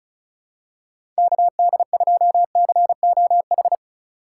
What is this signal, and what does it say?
Morse code, heard as a keyed tone, about 34 wpm, 700 Hz. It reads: KD2COH
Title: USA Call Signs - 6 Characters - 3x Character Spacing - 35wpm